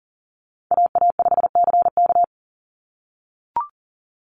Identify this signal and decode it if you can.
Morse code, heard as a keyed tone, about 41 wpm, 700 Hz. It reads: AA5CK
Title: NA SOTA Chaser Callsigns 40wpm